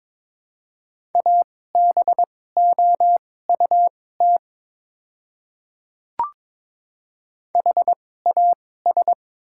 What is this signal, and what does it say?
Morse code, heard as a keyed tone, about 23 wpm, spaced as for 16 wpm, 700 Hz. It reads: ABOUT HAS
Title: 2x Character Spacing - Top 100 Words - 22wpm